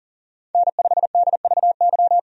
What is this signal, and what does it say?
Morse code, heard as a keyed tone, about 38 wpm, 700 Hz. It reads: N5DVY
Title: USA Call Signs - 5 Characters - 40wpm